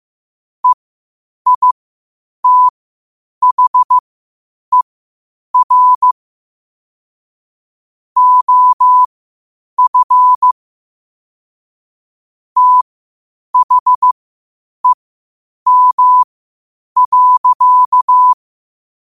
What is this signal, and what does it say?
Morse code, heard as a keyed tone, about 15 wpm, spaced as for 8 wpm, 1000 Hz. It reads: EITHER OF THEM.
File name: qotdfiles / 8wpm